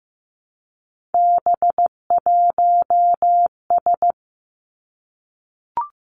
Morse code, heard as a keyed tone, about 15 wpm, 700 Hz. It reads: B1S
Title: Three Challenging Letter-Number Combinations 15wpm